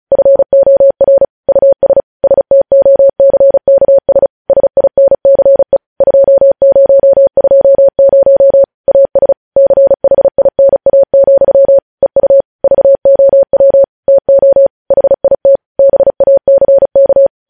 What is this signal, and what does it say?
Morse code, heard as a keyed tone, about 36 wpm, 550 Hz. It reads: FOR US STOCKS SINCE 2020 AS CHINA, EU VOW TO HIT BACK
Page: news headlines in morse code at 35 wpm